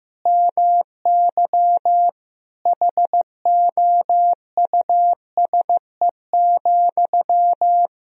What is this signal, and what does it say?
Morse code, heard as a keyed tone, about 15 wpm, 700 Hz. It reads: MY HOUSE,
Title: Sentences from Top 400 Words and Abbreviations 15wpm